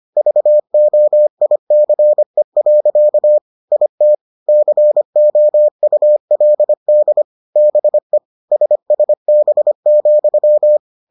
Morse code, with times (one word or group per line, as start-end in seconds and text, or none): 0.2-3.4 VOICE.
3.7-4.1 IT
4.5-7.2 COULD
7.6-8.2 BE
8.5-10.8 SSB,